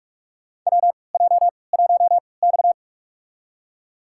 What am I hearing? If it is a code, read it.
Morse code, heard as a keyed tone, about 46 wpm, spaced as for 26 wpm, 700 Hz. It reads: WJ1X
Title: USA Call Signs - 4 Characters - 3x Character Spacing - 45wpm